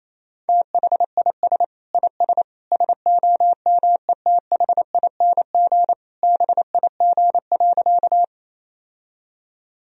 Morse code, heard as a keyed tone, about 28 wpm, 700 Hz. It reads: THIS IS SOMETHING BIG.
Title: Sentences from Top 1000 Words - Review Part 1 - 28wpm